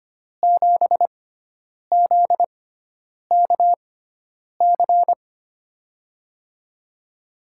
Morse code, heard as a keyed tone, about 25 wpm, spaced as for 9 wpm, 700 Hz. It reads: 7ZKC
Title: USA Call Signs - 5 Characters - 6x Character Spacing - 25wpm